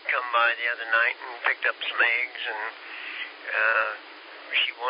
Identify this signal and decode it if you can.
Emergency Alert System (SAME) tones, heard as harsh buzzing bursts, with speech. It is heard from 0.5 to 3.8 s.